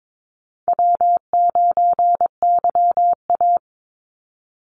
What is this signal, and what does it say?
Morse code, heard as a keyed tone, about 22 wpm, 700 Hz. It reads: W9YA
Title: Call Signs Repeated 2x - 22wpm